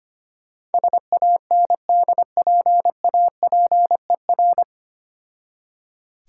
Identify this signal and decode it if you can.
Morse code, heard as a keyed tone, about 25 wpm, 700 Hz. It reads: SANDPAPER
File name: practice / 25wpm / Difficult Words with Hints - 25wpm